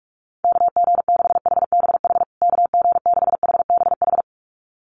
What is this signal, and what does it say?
Morse code, heard as a keyed tone, about 44 wpm, 700 Hz. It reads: KG6HBH KG6HBH
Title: USA Call Signs Repeated 2x 45wpm